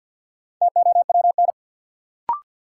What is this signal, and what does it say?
Morse code, heard as a keyed tone, about 52 wpm, 700 Hz. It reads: TOWN
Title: Sentences from Top 500 Words - New Word Drill - 50wpm